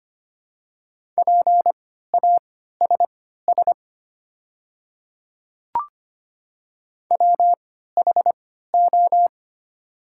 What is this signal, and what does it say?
Morse code, heard as a keyed tone, about 25 wpm, spaced as for 14 wpm, 700 Hz. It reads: PASS WHO
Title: 3x Character Spacing - Top 500 Words - 25wpm